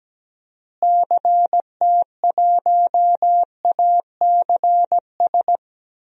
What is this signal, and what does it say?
Morse code, heard as a keyed tone, about 17 wpm, 700 Hz. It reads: CT1ACS
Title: DXCC-100 Call Signs - 6 Characters - Repeated 2x - 17wpm